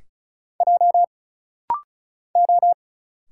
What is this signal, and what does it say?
Morse code, heard as a keyed tone, about 35 wpm, 700 Hz. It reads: J O
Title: Single Letters - Rapid-Fire 35wpm